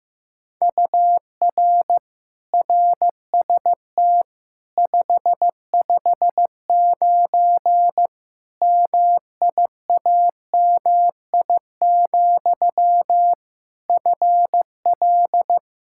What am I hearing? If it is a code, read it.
Morse code, heard as a keyed tone, about 15 wpm, 700 Hz. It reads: UR RST 559 MIAMI, FL
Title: K1USN SST QSO Practice 15wpm